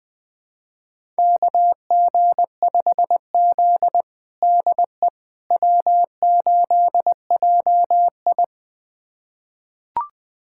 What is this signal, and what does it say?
Morse code, heard as a keyed tone, about 21 wpm, 700 Hz. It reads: KG5Z DE W8JI
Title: USA Callsign DE Callsign - 4 Characters - 20wpm